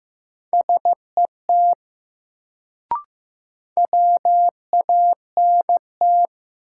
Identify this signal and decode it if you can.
Morse code, heard as a keyed tone, about 16 wpm, 700 Hz. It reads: SET WANT
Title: Sentences from Top 200 Words - New Word Drill - 15wpm